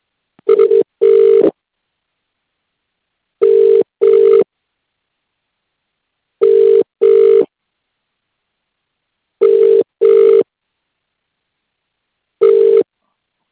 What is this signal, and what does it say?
Ringback tone (UK), 400 + 450 Hz.